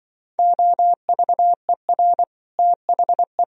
Morse code, heard as a keyed tone, about 24 wpm, 700 Hz. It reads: OVER THE
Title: Speed-Racing - Sets of 3 Words 20wpm